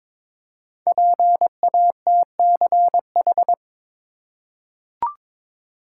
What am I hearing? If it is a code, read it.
Morse code, heard as a keyed tone, about 22 wpm, 700 Hz. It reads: PATCH
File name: 5 Letter Words - First 3 Letters Given - 22wpm